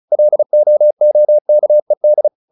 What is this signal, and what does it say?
Morse code, heard as a keyed tone, about 35 wpm, 600 Hz. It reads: LOOKED
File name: the old man and the sea 2 35wpm